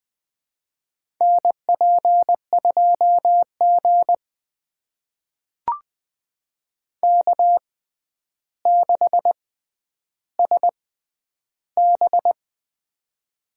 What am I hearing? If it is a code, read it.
Morse code, heard as a keyed tone, about 20 wpm, spaced as for 10 wpm, 700 Hz. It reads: NP2G K6SB
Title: USA Call Signs - 4 Characters - 6x Character Spacing - 20wpm